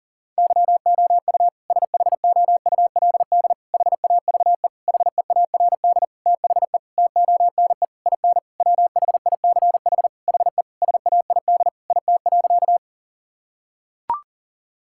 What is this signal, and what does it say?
Morse code, heard as a keyed tone, about 37 wpm, 700 Hz. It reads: YOU SHOULD HAVE HEARD THE TONE IN WHICH HE SAID IT.
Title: Sentences from Top 1600 Words 40wpm